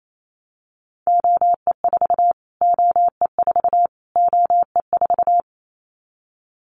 Morse code, heard as a keyed tone, about 28 wpm, 700 Hz. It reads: OE4 OE4 OE4
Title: Callsign Prefixes for Top 25 DXCC - Repeated 3x - 28wpm